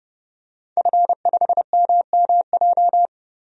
Morse code, heard as a keyed tone, about 29 wpm, 700 Hz. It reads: F5MMJ
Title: DXCC-100 Call Signs - 5 Characters - 30wpm